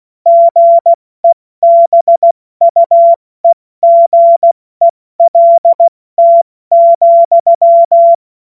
Morse code, heard as a keyed tone, about 16 wpm, 680 Hz. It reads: GEBUEGELT,
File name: Witz-09-September-2025-70-F80